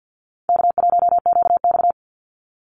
Morse code, heard as a keyed tone, about 50 wpm, 700 Hz. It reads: K1QX
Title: Easy Call Signs Repeated 2x - 50wpm